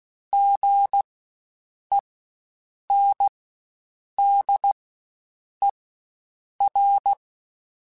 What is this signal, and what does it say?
Morse code, heard as a keyed tone, about 16 wpm, spaced as for 7 wpm, 800 Hz. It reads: GENDER
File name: Spaced Alphas / Spaced Alphas mp3s / Gs sp @15